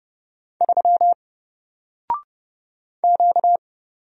Morse code, heard as a keyed tone, about 30 wpm, 700 Hz. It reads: 3 Q
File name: Single-Letter-Number-30wpm